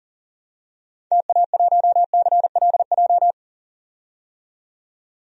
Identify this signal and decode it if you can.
Morse code, heard as a keyed tone, about 39 wpm, 700 Hz. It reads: TA1CLJ
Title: DXCC-100 Call Signs - 6 Characters - 40wpm